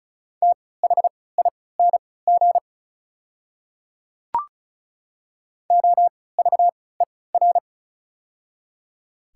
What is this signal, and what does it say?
Morse code, heard as a keyed tone, about 36 wpm, spaced as for 20 wpm, 700 Hz. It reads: THING OVER
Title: 3x Character Spacing - Top 100 Words - 35wpm